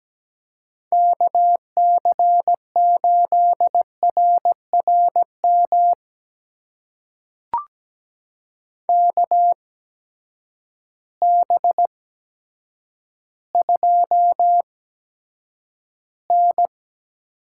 Morse code, heard as a keyed tone, about 17 wpm, spaced as for 8 wpm, 700 Hz. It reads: KC8RRM KB2N
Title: USA Call Signs - 6 Characters - 8x Character Spacing - 17wpm